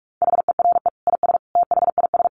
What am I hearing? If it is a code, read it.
Morse code, heard as a keyed tone, about 46 wpm, 700 Hz. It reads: HERE IS THIS
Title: Sentences from Top 1000 Words 45wpm